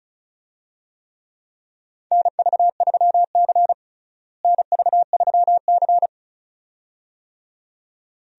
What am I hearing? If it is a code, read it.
Morse code, heard as a keyed tone, about 34 wpm, 700 Hz. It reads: NV3C NV3C
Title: USA Call Signs - 4 Characters - Repeated 2x - 35wpm